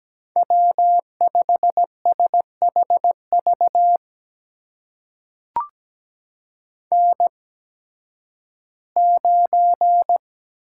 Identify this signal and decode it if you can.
Morse code, heard as a keyed tone, about 17 wpm, spaced as for 11 wpm, 700 Hz. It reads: W5SHV N9
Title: USA Call Signs - 5 Characters - 8x Character Spacing - 17wpm